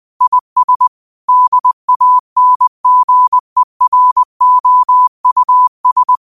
Morse code, heard as a keyed tone, about 20 wpm, 1000 Hz. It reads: IS DANGEROUS
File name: qotdfiles / 20wpm